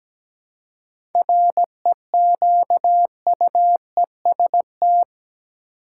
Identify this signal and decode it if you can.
Morse code, heard as a keyed tone, about 17 wpm, 700 Hz. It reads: REQUEST
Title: Seven Letter Words - 17wpm